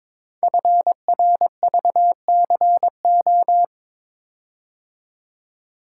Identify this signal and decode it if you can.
Morse code, heard as a keyed tone, about 22 wpm, 700 Hz. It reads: FRVCO
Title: Code Groups - 5 Letters - 22wpm